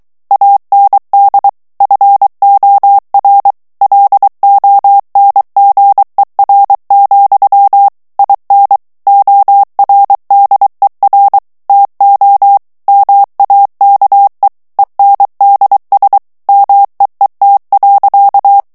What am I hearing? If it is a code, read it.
Morse code, heard as a keyed tone, about 23 wpm, 800 Hz. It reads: AND FOR LONGER, IN ORDER TO MAKE ENDS MEET.
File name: morse mp3 / ANGLAIS texte clair 1 vitesse 22 mots minute